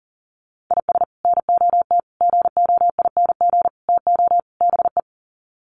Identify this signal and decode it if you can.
Morse code, heard as a keyed tone, about 40 wpm, 700 Hz. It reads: IS NOT GOING TO BE
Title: Sets of 5 Words Encore2 40wpm